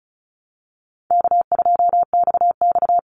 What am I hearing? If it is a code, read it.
Morse code, heard as a keyed tone, about 34 wpm, 700 Hz. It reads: K2XX
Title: Call Signs Repeated 2x - 35wpm